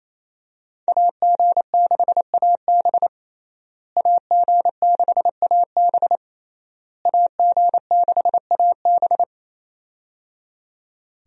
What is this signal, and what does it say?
Morse code, heard as a keyed tone, about 27 wpm, 700 Hz. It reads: AG6AB AG6AB AG6AB
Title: USA Call Signs - 5 Characters - Repeated 3x - 28wpm